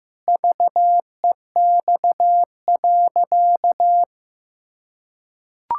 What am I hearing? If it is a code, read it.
Morse code, heard as a keyed tone, about 15 wpm, 700 Hz. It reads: VEX.
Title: Pangram Challenge v2 - 15wpm